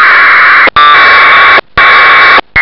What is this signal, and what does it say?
Packet radio (AX.25, as in APRS).